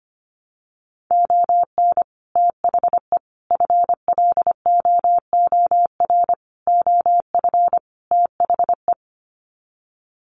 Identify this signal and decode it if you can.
Morse code, heard as a keyed tone, about 25 wpm, 700 Hz. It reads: ON THE FLOOR OF THE
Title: Sets of 5 Words 25wpm